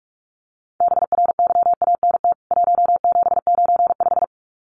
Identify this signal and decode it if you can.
Morse code, heard as a keyed tone, about 44 wpm, 700 Hz. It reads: BRYANT 1795
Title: CWT Exchange 45wpm